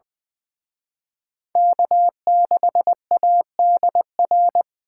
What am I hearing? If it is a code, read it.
Morse code, heard as a keyed tone, about 25 wpm, 700 Hz. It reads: K6ADR
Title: Speed Racing - USA 1x3 Call Signs 20wpm